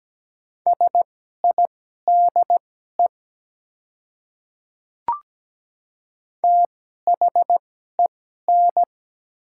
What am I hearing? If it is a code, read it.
Morse code, heard as a keyed tone, about 17 wpm, spaced as for 12 wpm, 690 Hz. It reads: SIDE THEN
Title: 2x Character Spacing - Top 100 Words - 17wpm